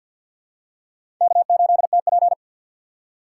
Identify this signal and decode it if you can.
Morse code, heard as a keyed tone, about 49 wpm, 700 Hz. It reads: K8TP
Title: USA Call Signs - 4 Characters - 2x Character Spacing - 50wpm